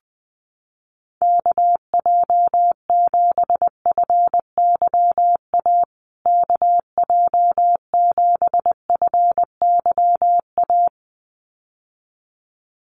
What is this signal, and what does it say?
Morse code, heard as a keyed tone, about 19 wpm, 700 Hz. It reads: KJ7FYA KJ7FYA
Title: USA Call Signs Repeated 2x 20wpm